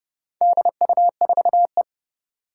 Morse code, heard as a keyed tone, about 30 wpm, 700 Hz. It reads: DU4E
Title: Four Challenging Letter-Number Combinations 30wpm